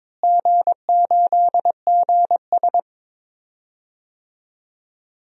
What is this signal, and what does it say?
Morse code, heard as a keyed tone, about 22 wpm, 700 Hz. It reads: G8GS
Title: DXCC-100 Call Signs - 4 Characters - Repeated 2x - 22wpm